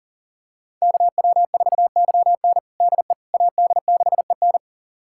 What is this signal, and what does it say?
Morse code, heard as a keyed tone, about 40 wpm, 700 Hz. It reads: KW4YN DE AD6EN
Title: USA Callsign DE Callsign - 5 Characters - 40wpm